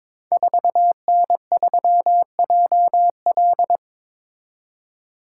Morse code, heard as a keyed tone, about 22 wpm, 700 Hz. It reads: 4N3JL
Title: Code Groups - 5 Letter:Numbers - 22wpm